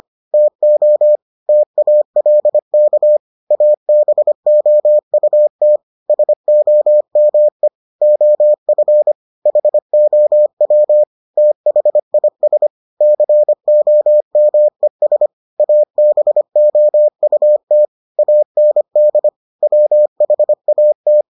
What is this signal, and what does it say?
Morse code, heard as a keyed tone, about 25 wpm, 600 Hz. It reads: TO TALK ABOUT SOME OF HOW THIS COMES ABOUT AND WHAT